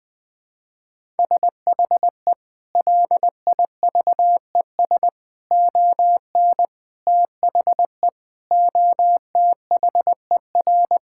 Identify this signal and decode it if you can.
Morse code, heard as a keyed tone, about 21 wpm, 700 Hz. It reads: SHE LIVES ON THE OTHER
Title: Sentences with 7-Word Phrases - Spoken Beginning - 20wpm